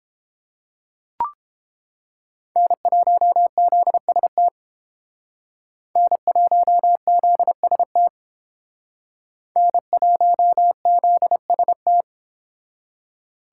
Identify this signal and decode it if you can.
Morse code, heard as a keyed tone, about 30 wpm, 700 Hz. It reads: N1ZST N1ZST N1ZST